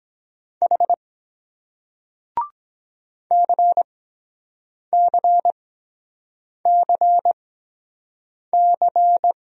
Morse code, heard as a keyed tone, about 23 wpm, spaced as for 7 wpm, 700 Hz. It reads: H CCCC